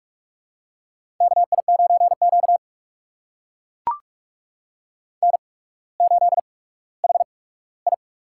Morse code, heard as a keyed tone, about 44 wpm, spaced as for 19 wpm, 700 Hz. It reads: KI9Q N8HI
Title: USA Call Signs - 8x Character Spacing - 45wpm